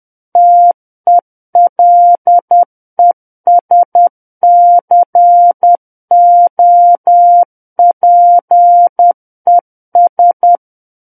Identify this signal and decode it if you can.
Morse code, heard as a keyed tone, about 10 wpm, 700 Hz. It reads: TELESCOPES
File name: Book One / 01-Chapter 1 - 10 WPM